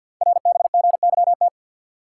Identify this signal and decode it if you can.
Morse code, heard as a keyed tone, about 50 wpm, 690 Hz. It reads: RBGYT
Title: Code Groups - 5 Letters - 50wpm